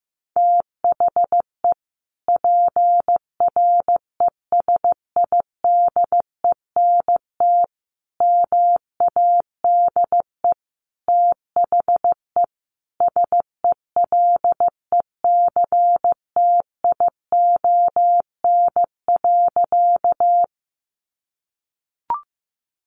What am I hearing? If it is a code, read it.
Morse code, heard as a keyed tone, about 15 wpm, 700 Hz. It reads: THE PRESIDENT MADE THE SELECTION.
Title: Sentences from Top 1400 Words 15wpm